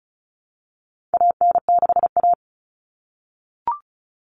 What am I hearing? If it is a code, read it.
Morse code, heard as a keyed tone, about 34 wpm, 700 Hz. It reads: AN6A
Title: All Cut Numbers - 4-Digit 35wpm